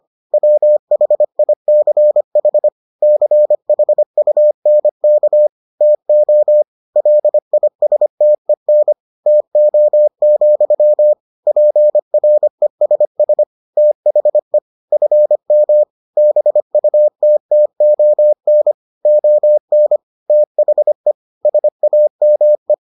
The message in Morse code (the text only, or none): WHICH CHUNK TO LISTEN TO, PRESS THE FM BUTTON ON THE SAME